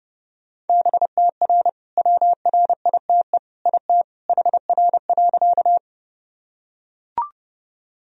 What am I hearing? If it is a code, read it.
Morse code, heard as a keyed tone, about 30 wpm, 700 Hz. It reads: BTR WRITE IT HR.
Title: Sentences from Top 400 Words and Abbreviations 30wpm